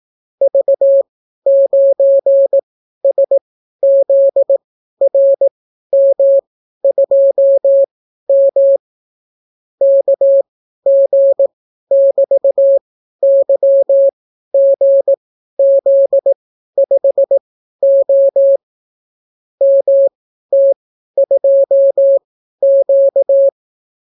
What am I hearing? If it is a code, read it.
Morse code, heard as a keyed tone, about 18 wpm, spaced as for 12 wpm, 550 Hz. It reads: V9SZRM2M KG<BT>YGZ5O MT2Q